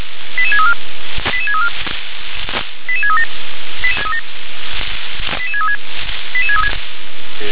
Five-tone selective call (ZVEI), heard as quick runs of five tones, tones of about 80 ms. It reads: 88634 88634 88637 88636 88636
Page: SDR/ZVEI